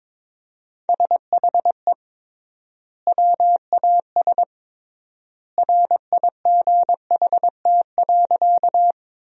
Morse code, heard as a keyed tone, about 22 wpm, 700 Hz. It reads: SHE WAS RIGHT.
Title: 3x Word-Spacing - Sentences from Top 200 Words - 22wpm